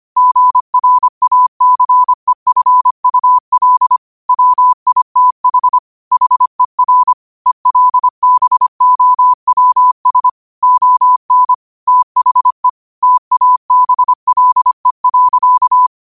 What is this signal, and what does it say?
Morse code, heard as a keyed tone, about 25 wpm, 1000 Hz. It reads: GRACEFUL WITH HER ELBOWS ON THE TABLE.
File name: qotdfiles / 25wpm